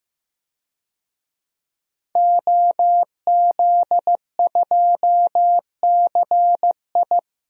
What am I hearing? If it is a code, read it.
Morse code, heard as a keyed tone, about 15 wpm, 700 Hz. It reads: OZ2CI
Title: DXCC-100 Call Signs - 5 Characters - 15wpm